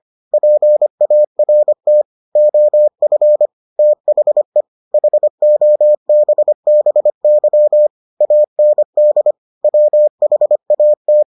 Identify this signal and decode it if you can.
Morse code, heard as a keyed tone, about 25 wpm, 600 Hz. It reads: PART OF THE HOBBY AND WHAT